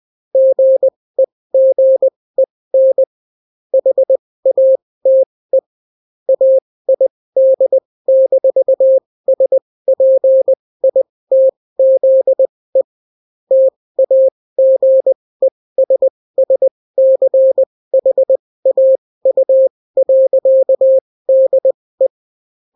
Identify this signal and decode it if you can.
Morse code, heard as a keyed tone, about 20 wpm, spaced as for 16 wpm, 530 Hz. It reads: GEGEN HATE AID-SPITZE TAGESSCHAU.DE